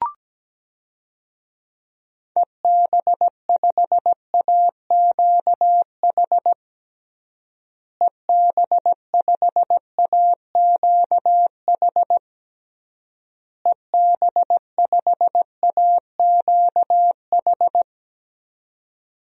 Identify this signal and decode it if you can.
Morse code, heard as a keyed tone, about 17 wpm, 700 Hz. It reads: EB5AQH EB5AQH EB5AQH
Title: DXCC-100 Call Signs - 6 Characters - Repeated 3x - 17wpm